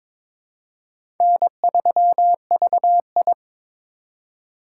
Morse code, heard as a keyed tone, about 21 wpm, 700 Hz. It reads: N3VI
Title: USA Call Signs - 4 Characters - 2x Character Spacing - 22wpm